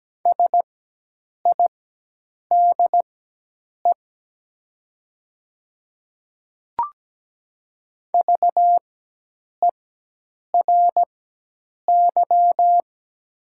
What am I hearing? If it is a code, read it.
Morse code, heard as a keyed tone, about 17 wpm, spaced as for 8 wpm, 700 Hz. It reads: SIDE VERY
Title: 4x Character Spacing - Top 200 Words - 17wpm